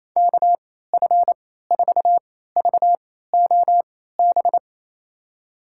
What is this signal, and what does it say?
Morse code, heard as a keyed tone, about 28 wpm, spaced as for 16 wpm, 700 Hz. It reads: KF4VOB
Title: USA Call Signs - 6 Characters - 3x Character Spacing - 28wpm